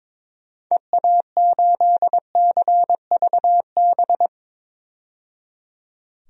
Morse code, heard as a keyed tone, about 22 wpm, 700 Hz. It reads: EA8CVB
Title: DXCC-100 Call Signs - 6 Characters - 22wpm